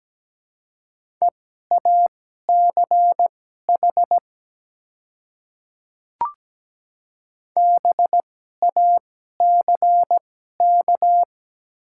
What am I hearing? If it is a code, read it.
Morse code, heard as a keyed tone, about 17 wpm, spaced as for 12 wpm, 700 Hz. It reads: EACH BACK
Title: 2x Character Spacing - Top 200 Words - 17wpm